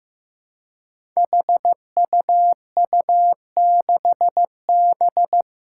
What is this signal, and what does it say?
Morse code, heard as a keyed tone, about 15 wpm, 700 Hz. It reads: HUU6B
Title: Code Groups - 5 Letter:Numbers - 15wpm